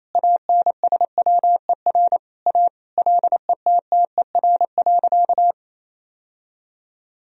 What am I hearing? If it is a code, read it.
Morse code, heard as a keyed tone, about 28 wpm, 700 Hz. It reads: ANSWER A LETTER.
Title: Sentences from Top 1000 Words - Review Part 1 - 28wpm